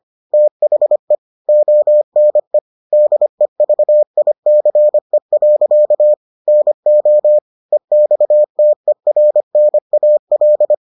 Morse code, heard as a keyed tone, about 25 wpm, 600 Hz. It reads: THE ONE DEVICE. NO EXTERNAL